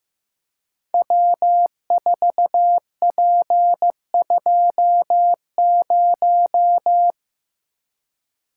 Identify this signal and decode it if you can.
Morse code, heard as a keyed tone, about 15 wpm, 700 Hz. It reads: W4P20
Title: Code Groups - 5 Letter:Numbers - 15wpm